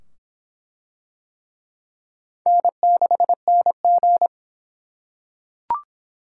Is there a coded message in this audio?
Yes: Morse code from 2.5 to 4.3 s, heard as a keyed tone, about 25 wpm, 700 Hz. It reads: N6NG